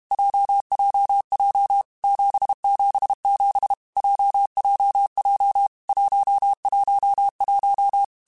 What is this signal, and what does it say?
Morse code, heard as a keyed tone, about 32 wpm, 800 Hz. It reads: JJJ 777 JJJ 111
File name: Basic Letter Groups By7Groups / JZ17 / JZ17@30